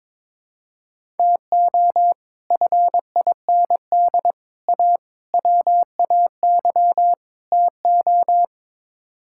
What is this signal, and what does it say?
Morse code, heard as a keyed tone, about 22 wpm, 700 Hz. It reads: TO FIND A WAY TO
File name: practice / 22wpm / Build A Phrase - 5-Word Phrases - 22wpm